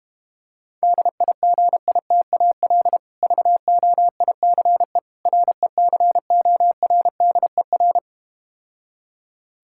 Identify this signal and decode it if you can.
Morse code, heard as a keyed tone, about 32 wpm, 700 Hz. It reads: DIGITAL VOICE RECORDER